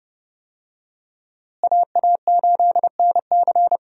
Morse code, heard as a keyed tone, about 31 wpm, 700 Hz. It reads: AA8NC
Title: USA Call Signs - 5 Characters - Repeated 3x - 30wpm